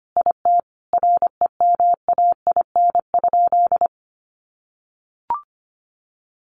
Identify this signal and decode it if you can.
Morse code, heard as a keyed tone, about 25 wpm, 700 Hz. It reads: IT REMAIN?
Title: Sentences from Top 1200 Words 25wpm